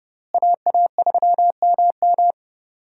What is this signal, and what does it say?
Morse code, heard as a keyed tone, about 31 wpm, 700 Hz. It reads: AA3MM
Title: USA Call Signs - 5 Characters - Repeated 3x - 30wpm